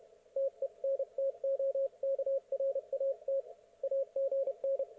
Morse code, heard as a keyed tone, about 31 wpm, 530 Hz, with radio noise. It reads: TENTOKRAT AGN
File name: Archiv / CUC107 / ok1fhi cul107